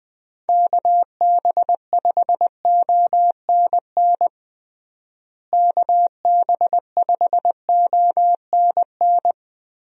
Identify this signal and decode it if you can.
Morse code, heard as a keyed tone, about 21 wpm, 700 Hz. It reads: KB5ONN KB5ONN
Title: USA Call Signs - 6 Characters - Repeated 3x - 20wpm